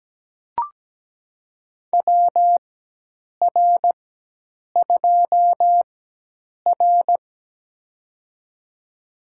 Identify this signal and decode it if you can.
Morse code, heard as a keyed tone, about 17 wpm, spaced as for 8 wpm, 700 Hz. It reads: WR2R
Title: USA Call Signs - 4 Characters - 4x Character Spacing - 17wpm